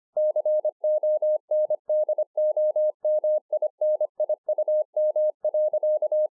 Morse code, heard as a keyed tone, about 24 wpm, 610 Hz. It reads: CONDOMINIUM.
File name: 2011-04-mp3 / BEGINNERS DIARY 25WPM